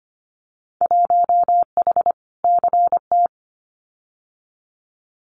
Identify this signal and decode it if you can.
Morse code, heard as a keyed tone, about 25 wpm, 700 Hz. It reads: 1H CT
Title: Winter Field Day Exchanges Encore 25wpm